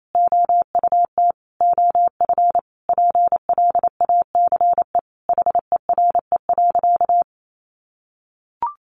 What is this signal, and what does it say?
Morse code, heard as a keyed tone, about 28 wpm, 700 Hz. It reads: OUT OF PLACE HERE.
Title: Sentences from Top 1500 Words 28wpm